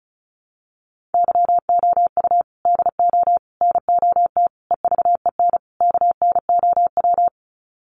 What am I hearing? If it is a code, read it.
Morse code, heard as a keyed tone, about 34 wpm, 700 Hz. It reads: YOU DO NOT EVEN KNOW
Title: Sets of 5 Words Encore2 35wpm